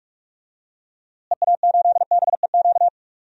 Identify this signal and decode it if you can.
Morse code, heard as a keyed tone, about 47 wpm, 700 Hz. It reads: EA8BEQ
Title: DXCC-100 Call Signs - 6 Characters - Repeated 2x - 45wpm